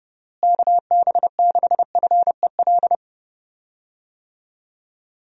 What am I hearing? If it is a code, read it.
Morse code, heard as a keyed tone, about 30 wpm, 700 Hz. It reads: KB6FEL
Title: USA Call Signs - 6 Characters - Repeated 2x - 30wpm